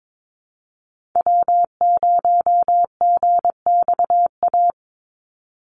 Morse code, heard as a keyed tone, about 22 wpm, 700 Hz. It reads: W0GXA
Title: Call Signs Repeated 2x - 22wpm